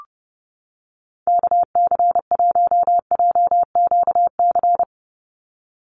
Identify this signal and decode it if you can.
Morse code, heard as a keyed tone, about 30 wpm, 700 Hz. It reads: KC1JQC
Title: USA Call Signs 30wpm